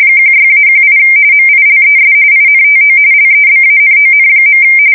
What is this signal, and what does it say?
RTTY, about 45.45 baud. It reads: 16-18 MILES INTO LEBANON.
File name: fsk170 / battleshipnj